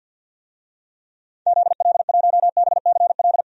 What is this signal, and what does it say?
Morse code, heard as a keyed tone, about 51 wpm, 700 Hz. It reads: ZL1BCL
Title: DXCC-100 Call Signs - 6 Characters - 50wpm